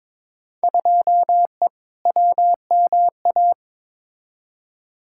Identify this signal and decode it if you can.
Morse code, heard as a keyed tone, about 22 wpm, 700 Hz. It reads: 2E WMA
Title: Field Day Exchanges Encore - 22wpm